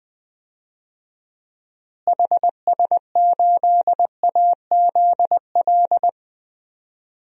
Morse code, heard as a keyed tone, about 20 wpm, 700 Hz. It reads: HS8AZL
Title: DXCC-100 Call Signs - 6 Characters - Repeated 2x - 20wpm